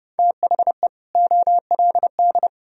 Morse code, heard as a keyed tone, about 28 wpm, 700 Hz. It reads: THE OLD